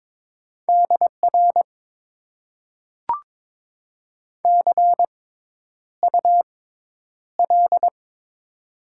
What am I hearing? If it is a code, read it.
Morse code, heard as a keyed tone, about 22 wpm, spaced as for 10 wpm, 700 Hz. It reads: DR CUL
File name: Top 90 QSO Elements - 6x Character Spacing - 22wpm